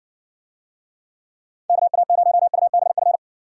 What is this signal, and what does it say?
Morse code, heard as a keyed tone, about 60 wpm, 700 Hz. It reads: KA0UBF